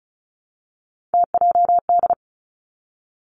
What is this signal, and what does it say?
Morse code, heard as a keyed tone, about 36 wpm, 700 Hz. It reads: TJD
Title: Three Challenging Letter-Number Combinations 35wpm